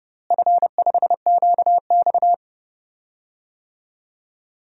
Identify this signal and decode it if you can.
Morse code, heard as a keyed tone, about 30 wpm, 700 Hz. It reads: F5QX
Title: DXCC-100 Call Signs - 4 Characters - Repeated 2x - 30wpm